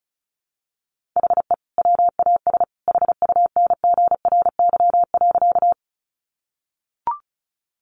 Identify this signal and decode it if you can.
Morse code, heard as a keyed tone, about 34 wpm, 700 Hz. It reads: HE WAS HUNGRY.
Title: Sentences from Top 1400 Words 35wpm